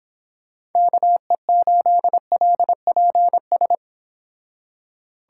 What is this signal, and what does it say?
Morse code, heard as a keyed tone, about 29 wpm, 700 Hz. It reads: KE8LPS